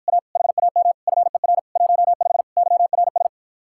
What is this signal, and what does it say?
Morse code, heard as a keyed tone, about 52 wpm, 700 Hz. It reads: A HAM FER 15 YRS